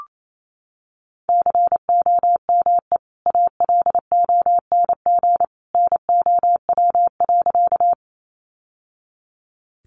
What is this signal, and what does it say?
Morse code, heard as a keyed tone, about 28 wpm, 700 Hz. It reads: COME ALONG NOW.
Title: Sentences from Top 700 Words 28wpm